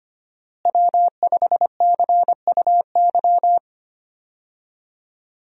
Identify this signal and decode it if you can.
Morse code, heard as a keyed tone, about 25 wpm, 700 Hz. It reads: W5CUY
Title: USA Call Signs - 5 Characters - Repeated 2x - 25wpm